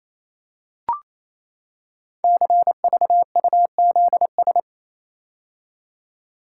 Morse code, heard as a keyed tone, about 28 wpm, 700 Hz. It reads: CVUZS
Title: Code Groups - 5 Letters - 28wpm